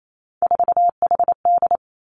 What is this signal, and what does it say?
Morse code, heard as a keyed tone, about 28 wpm, 700 Hz. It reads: 4HD
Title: Three Challenging Letter-Number Combinations 28wpm